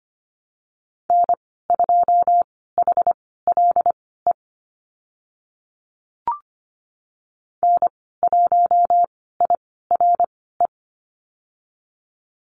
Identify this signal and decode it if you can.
Morse code, heard as a keyed tone, about 25 wpm, spaced as for 16 wpm, 700 Hz. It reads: N2HLE N1IRE
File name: Lesson 15g - Number 5 - Call Signs 25wpm